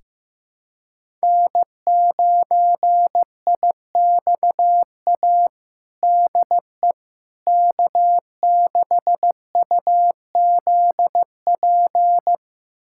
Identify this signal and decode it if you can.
Morse code, heard as a keyed tone, about 15 wpm, 700 Hz. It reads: N9IXA DE K6UZP
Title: USA Callsign DE Callsign - 5 Characters - 15wpm